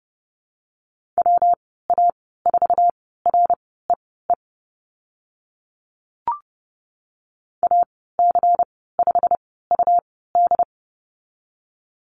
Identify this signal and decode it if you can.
Morse code, heard as a keyed tone, about 30 wpm, spaced as for 17 wpm, 700 Hz. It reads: WA4REE AC5UD
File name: Lesson 19g - Letter w - Call Signs 30wpm